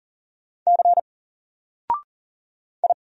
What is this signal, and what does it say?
Morse code, heard as a keyed tone, about 40 wpm, 700 Hz. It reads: C I
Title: Single Letter-Number Alt 40wpm